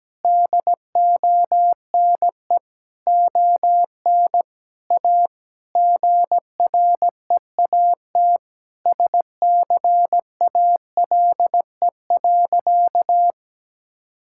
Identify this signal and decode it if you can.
Morse code, heard as a keyed tone, about 17 wpm, 700 Hz. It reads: DONE ON A GREAT SCALE.
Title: Sentences from Top 1000 Words - Review Part 5 - 17wpm